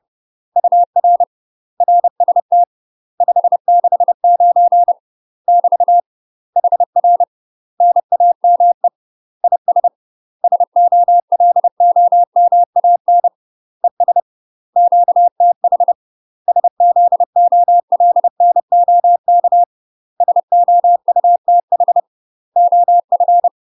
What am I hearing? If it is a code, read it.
Morse code, heard as a keyed tone, about 30 wpm, 700 Hz. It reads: UR RST 569 <BT> HR NAME IS SOLOMAN ES QTH SZOLNOK SOUTH OF
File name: UK MASTER QSOs TRAINING/30WPM